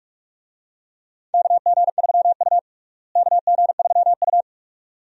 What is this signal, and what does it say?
Morse code, heard as a keyed tone, about 46 wpm, 700 Hz. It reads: KG3U KG3U
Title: USA Call Signs - 4 Characters - Repeated 2x - 45wpm